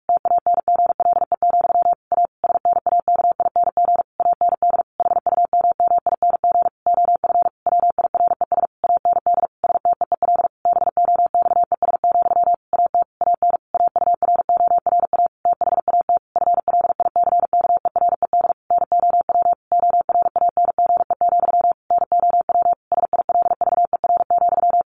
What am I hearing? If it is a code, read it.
Morse code, heard as a keyed tone, about 44 wpm, 700 Hz. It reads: TANGLE, A SNAKING AND HUMMING OF WIRES AND STEEL BOXES, AT AN AURORA THAT FLICKERED NOW ORANGE, NOW SILVER,
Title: A Sound Of Thunder 45wpm